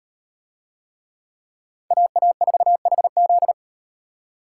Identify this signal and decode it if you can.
Morse code, heard as a keyed tone, about 38 wpm, 700 Hz. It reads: AA4HZ